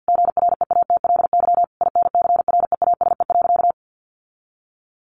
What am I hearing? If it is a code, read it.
Morse code, heard as a keyed tone, about 50 wpm, 700 Hz. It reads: GREATLY INCREASE.
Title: Sentences from Top 1400 Words 50wpm